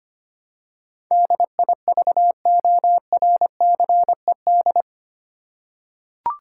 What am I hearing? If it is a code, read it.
Morse code, heard as a keyed tone, about 25 wpm, 700 Hz. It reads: DIVORCED
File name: Eight Letter Words - 25wpm